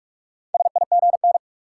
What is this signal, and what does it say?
Morse code, heard as a keyed tone, about 42 wpm, 700 Hz. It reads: SIGN
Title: Sentences from Top 1000 Words - New Word Drill - 45wpm